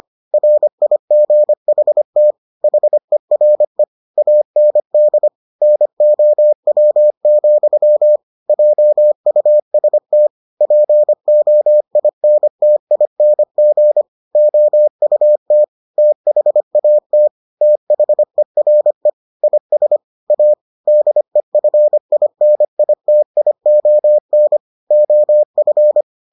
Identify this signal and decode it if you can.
Morse code, heard as a keyed tone, about 25 wpm, 600 Hz. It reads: RIGHT HERE AND NOW, JUST POINTING OUT THAT THERE IS A DEFINITION OF